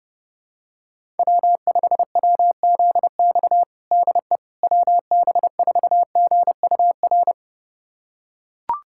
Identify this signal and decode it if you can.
Morse code, heard as a keyed tone, about 30 wpm, 700 Hz. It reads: W5WZX DE WB4GUR
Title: USA Callsign DE Callsign 30wpm